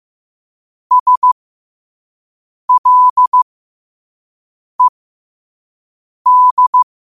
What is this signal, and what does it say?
Morse code, heard as a keyed tone, about 15 wpm, spaced as for 5 wpm, 1000 Hz. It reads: SLED
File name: qotdfiles / 5wpm